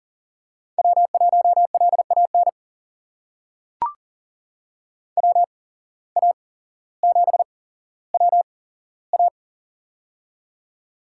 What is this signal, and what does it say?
Morse code, heard as a keyed tone, about 40 wpm, spaced as for 17 wpm, 700 Hz. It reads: W1LAN WA7WA
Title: USA Call Signs - 5 Characters - 8x Character Spacing - 40wpm